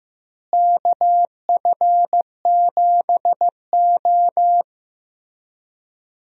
Morse code, heard as a keyed tone, about 15 wpm, 700 Hz. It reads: KF7O
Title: USA Call Signs - 4 Characters - Repeated 3x - 15wpm